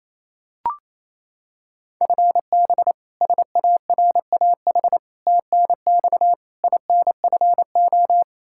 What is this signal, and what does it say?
Morse code, heard as a keyed tone, about 28 wpm, 700 Hz. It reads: FB SARAH TNX INFO
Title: K1USN SST QSO Practice 28wpm